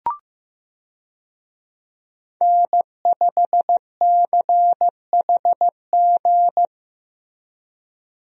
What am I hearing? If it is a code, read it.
Morse code, heard as a keyed tone, about 15 wpm, 700 Hz. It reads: N5CHG
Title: USA Call Signs - 5 Characters - Repeated 2x - 15wpm